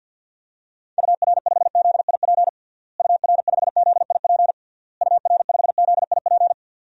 Morse code, heard as a keyed tone, about 52 wpm, 700 Hz. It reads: UR5ZIP UR5ZIP UR5ZIP
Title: DXCC-100 Call Signs - 6 Characters - Repeated 3x - 50wpm